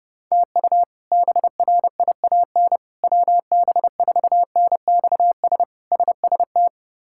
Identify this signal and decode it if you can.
Morse code, heard as a keyed tone, about 32 wpm, 700 Hz. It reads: TU BRIAN WB4NXS SST
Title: K1USN SST QSO Practice - 30wpm